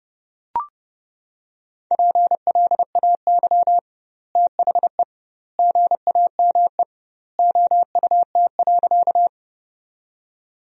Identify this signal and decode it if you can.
Morse code, heard as a keyed tone, about 32 wpm, 700 Hz. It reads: PLAY THE GAME OUT.
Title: 2x Word-Spacing - Sentences from Top 500 Words - 30wpm